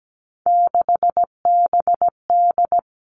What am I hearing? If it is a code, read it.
Morse code, heard as a keyed tone, about 17 wpm, 700 Hz. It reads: 6BD
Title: Three Challenging Letter-Number Combinations 17wpm